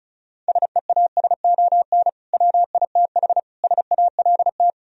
Morse code, heard as a keyed tone, about 46 wpm, spaced as for 41 wpm, 700 Hz. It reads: SEASON WITH SALT
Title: Speed-Racing - Sets of 5 Words 35wpm